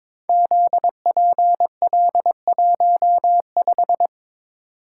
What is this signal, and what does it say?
Morse code, heard as a keyed tone, about 23 wpm, 700 Hz. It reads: ZPL15
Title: Code Groups - 5 Letter:Numbers - 22wpm